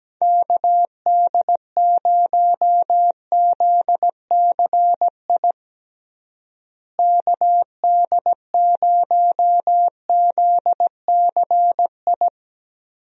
Morse code, heard as a keyed tone, about 17 wpm, 700 Hz. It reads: KD0ZCI KD0ZCI
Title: USA Call Signs - 6 Characters - Repeated 3x - 17wpm